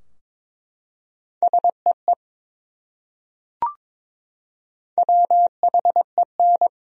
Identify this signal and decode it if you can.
Morse code, heard as a keyed tone, about 21 wpm, 700 Hz. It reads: SEE WHEN
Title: Top 100 Words Repeated 22wpm